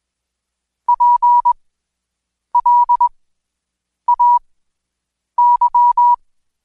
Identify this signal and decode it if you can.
Morse code, heard as a keyed tone, about 20 wpm, spaced as for 7 wpm, 970 Hz. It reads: PLAY
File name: Code Quick / AudioDownld2 / Seg112-Day17-DayDrill 17